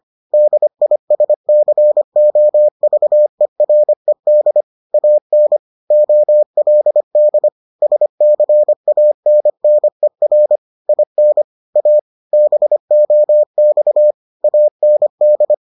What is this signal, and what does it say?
Morse code, heard as a keyed tone, about 25 wpm, 600 Hz. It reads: DISCOVERED AN OLD SCANNER IN A BOX AND